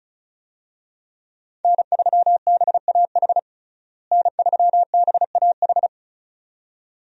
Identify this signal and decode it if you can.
Morse code, heard as a keyed tone, about 34 wpm, 700 Hz. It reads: N3BAH N3BAH
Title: USA Call Signs - 5 Characters - Repeated 2x - 35wpm